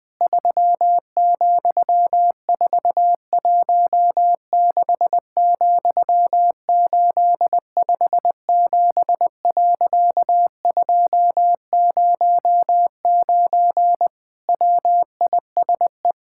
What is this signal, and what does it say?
Morse code, heard as a keyed tone, about 20 wpm, 700 Hz. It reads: 3,416,857.209 WISE
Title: Pangram Challenge v2 - 20wpm